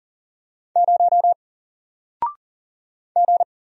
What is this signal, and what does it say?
Morse code, heard as a keyed tone, about 39 wpm, 700 Hz. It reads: 0 G
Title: Single Character (Letter:Number:Punctuation:Prosign) 40wpm